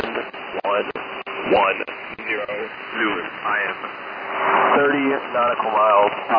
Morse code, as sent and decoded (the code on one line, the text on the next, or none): none